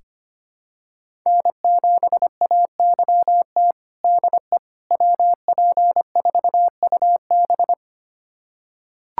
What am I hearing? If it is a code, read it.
Morse code, heard as a keyed tone, about 25 wpm, 700 Hz. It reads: N7AYT DE WP4UB
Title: USA Callsign DE Callsign - 5 Characters - 25wpm